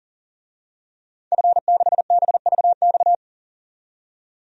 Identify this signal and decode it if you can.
Morse code, heard as a keyed tone, about 40 wpm, 700 Hz. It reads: F6BVX
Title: DXCC-100 Call Signs - 5 Characters - 40wpm